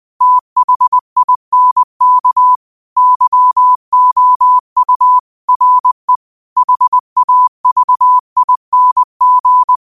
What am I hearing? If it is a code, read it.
Morse code, heard as a keyed tone, about 20 wpm, 1000 Hz. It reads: THINK YOU RE HAVING